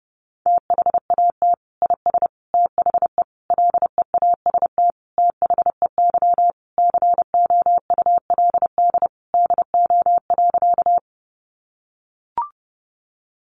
Morse code, heard as a keyed tone, about 30 wpm, 700 Hz. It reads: THAT IS THE LEAST THEY COULD DO.
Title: Sentences from Top 700 Words 30wpm